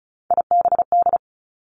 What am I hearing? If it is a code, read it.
Morse code, heard as a keyed tone, about 36 wpm, 700 Hz. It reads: IBD
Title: Three Challenging Letter-Number Combinations 35wpm